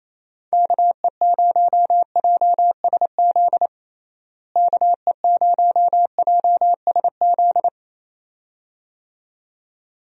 Morse code, heard as a keyed tone, about 28 wpm, 700 Hz. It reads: KE0JSZ KE0JSZ
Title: USA Call Signs - 6 Characters - Repeated 2x - 28wpm